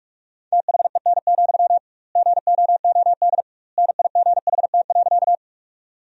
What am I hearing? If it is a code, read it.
Morse code, heard as a keyed tone, about 45 wpm, 700 Hz. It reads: THEN, GOOD NIGHT.
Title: 2x Word-Spacing - Sentences from Top 300 Words - 45wpm